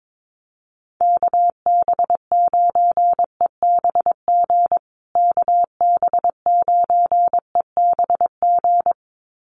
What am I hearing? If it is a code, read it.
Morse code, heard as a keyed tone, about 22 wpm, 700 Hz. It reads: KB9EBG KB9EBG
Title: USA Call Signs Repeated 2x 22wpm